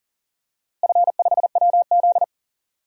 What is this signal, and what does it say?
Morse code, heard as a keyed tone, about 40 wpm, 700 Hz. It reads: F5WZ
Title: DXCC-100 Call Signs - 4 Characters - Repeated 3x - 40wpm